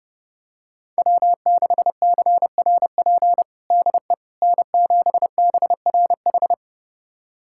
Morse code, heard as a keyed tone, about 30 wpm, 700 Hz. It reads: W6CRP DE N7BRH
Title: USA Callsign DE Callsign - 5 Characters - 30wpm